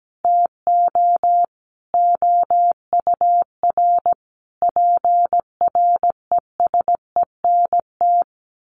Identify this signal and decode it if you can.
Morse code, heard as a keyed tone, about 17 wpm, 700 Hz. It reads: TO OUR PRESENT
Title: Sentences from Top 1400 Words 17wpm